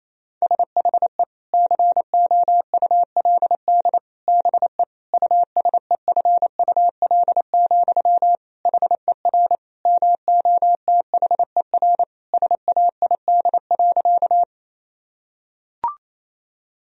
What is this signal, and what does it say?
Morse code, heard as a keyed tone, about 28 wpm, 700 Hz. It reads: SHE COULD BE USEFUL, HER MOTHER SAID.
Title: Sentences from Top 1000 Words - Review Part 4 - 28wpm